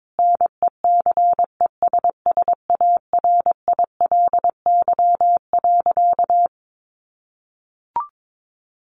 Morse code, heard as a keyed tone, about 22 wpm, 700 Hz. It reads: NECESSARILY.
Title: Sentences from Top 1400 Words 22wpm